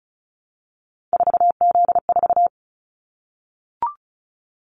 Morse code, heard as a keyed tone, about 36 wpm, 690 Hz. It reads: 4Z4
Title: Callsign Prefixes for Top 50 DXCC - 35wpm